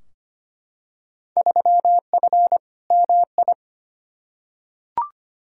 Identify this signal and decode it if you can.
Morse code, heard as a keyed tone, about 25 wpm, 700 Hz. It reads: 3F MI
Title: Field Day Exchanges - 25wpm